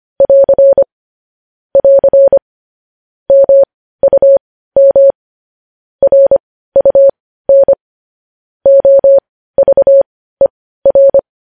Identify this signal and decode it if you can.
Morse code, heard as a keyed tone, about 25 wpm, spaced as for 15 wpm, 550 Hz. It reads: <AR> <AR> MUM RUN OVER
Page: news-headlines-in-morse-code-at-15-wpm